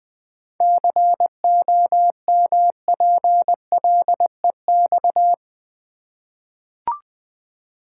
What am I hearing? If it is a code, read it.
Morse code, heard as a keyed tone, about 20 wpm, 700 Hz. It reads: COMPLEX
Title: Seven Letter Words - 20wpm